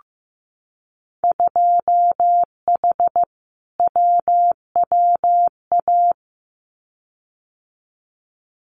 Morse code, heard as a keyed tone, about 15 wpm, 700 Hz. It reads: 2H WWA
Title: Winter Field Day Exchanges 15wpm